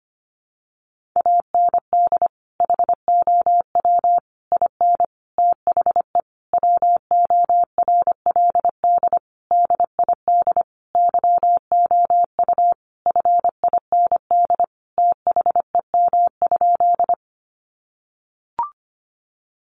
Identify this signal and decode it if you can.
Morse code, heard as a keyed tone, about 25 wpm, 700 Hz. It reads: AND HOW IN THE WORLD DID YOU FIND THEM?
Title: Sentences from Top 200 Words 25wpm